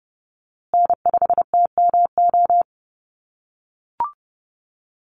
Morse code, heard as a keyed tone, about 30 wpm, 700 Hz. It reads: N5TMO
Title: USA Call Signs Repeated 3x 30wpm